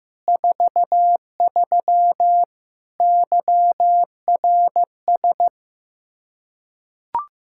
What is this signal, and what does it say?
Morse code, heard as a keyed tone, about 15 wpm, 700 Hz. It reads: 43 YRS
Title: Practice QSO - Exchange 3 - More Info - 15wpm